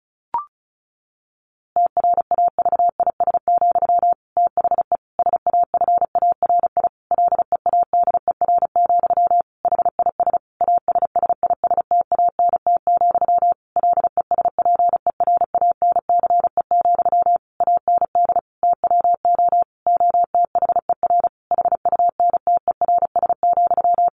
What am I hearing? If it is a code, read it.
Morse code, heard as a keyed tone, about 34 wpm, 700 Hz. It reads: TRAVIS, THE SAFARI LEADER, HIS ASSISTANT, LESPERANCE, AND TWO OTHER HUNTERS,
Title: A Sound Of Thunder 35wpm